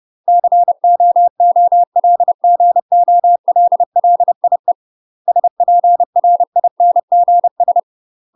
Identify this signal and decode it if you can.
Morse code, heard as a keyed tone, about 30 wpm, 700 Hz. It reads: COOLGOLLIE SPRINGS
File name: UK MASTER QSOs TRAINING / 30WPM / UK MASTER QSOs 0018